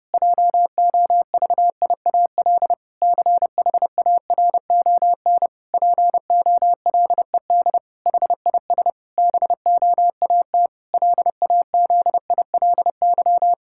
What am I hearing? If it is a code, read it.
Morse code, heard as a keyed tone, about 30 wpm, 700 Hz. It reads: JOVIAL CHARON POLED HIS BOAT LAZILY
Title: Pangram Challenge v2 - 30wpm